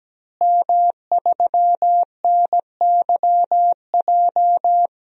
Morse code, heard as a keyed tone, about 17 wpm, 700 Hz. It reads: M3NYJ
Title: DXCC-100 Call Signs - 5 Characters - Repeated 2x - 17wpm